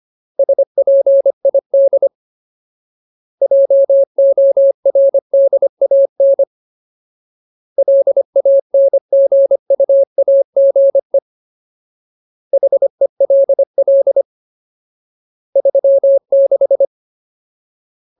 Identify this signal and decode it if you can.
Morse code, heard as a keyed tone, about 25 wpm, 550 Hz. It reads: SPID JORDAN LANGUAGE HELL 36
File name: Ultimate Word List / 25WPM Ultimate